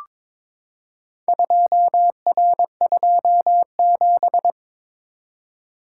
Morse code, heard as a keyed tone, about 21 wpm, 700 Hz. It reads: 2R27
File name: practice / 22wpm / Four Letter-Number Combinations - 22wpm